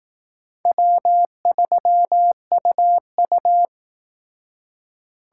Morse code, heard as a keyed tone, about 18 wpm, 700 Hz. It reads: W3UU
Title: Speed Racing - USA 1x2 Call Signs 15wpm